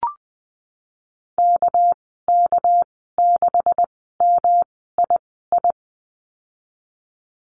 Morse code, heard as a keyed tone, about 20 wpm, spaced as for 15 wpm, 700 Hz. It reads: KK6MII
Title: 207 Lesson 33g - Letter j - Call Signs 20wpm